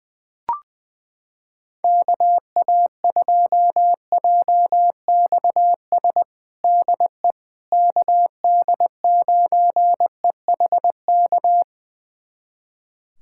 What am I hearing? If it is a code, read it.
Morse code, heard as a keyed tone, about 21 wpm, 700 Hz. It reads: KA2JXS DE KD9EHK
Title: USA Callsign DE Callsign - 6 Characters - 20wpm